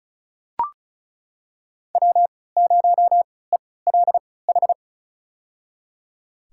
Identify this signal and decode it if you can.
Morse code, heard as a keyed tone, about 34 wpm, spaced as for 20 wpm, 690 Hz. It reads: W0ELH
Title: USA Call Signs - 5 Characters - 3x Character Spacing - 35wpm